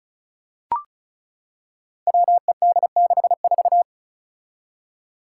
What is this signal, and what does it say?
Morse code, heard as a keyed tone, about 36 wpm, 700 Hz. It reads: WED64
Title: Five Challenging Letter-Number Combinations 35wpm